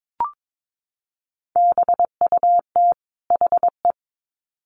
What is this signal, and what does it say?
Morse code, heard as a keyed tone, about 22 wpm, 700 Hz. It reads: BUT HE
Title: Sets of 2 Words 22wpm